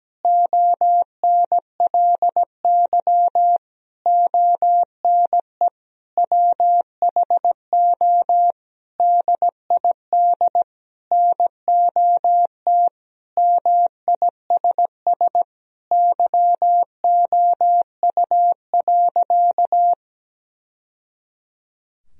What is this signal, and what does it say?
Morse code, heard as a keyed tone, about 17 wpm, 700 Hz. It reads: ONLY ONE WHO DID NOT MISS YOU.
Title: Sentences from Top 1000 Words - Review Part 3 - 17wpm